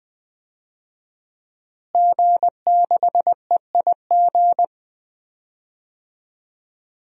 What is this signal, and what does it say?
Morse code, heard as a keyed tone, about 20 wpm, 700 Hz. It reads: G6EIG
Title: DXCC-100 Call Signs - 5 Characters - Repeated 3x - 20wpm